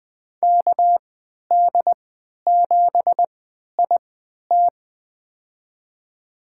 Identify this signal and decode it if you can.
Morse code, heard as a keyed tone, about 21 wpm, spaced as for 12 wpm, 700 Hz. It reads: KD7IT
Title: USA Call Signs - 5 Characters - 3x Character Spacing - 20wpm